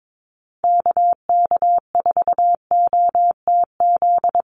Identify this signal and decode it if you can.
Morse code, heard as a keyed tone, about 22 wpm, 700 Hz. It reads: KK4OTZ
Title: USA Call Signs Repeated 3x 22wpm